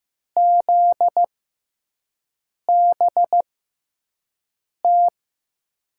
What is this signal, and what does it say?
Morse code, heard as a keyed tone, about 15 wpm, spaced as for 5 wpm, 700 Hz. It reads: ZBT
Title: USA Call Signs - 5 Characters - 6x Character Spacing - 15wpm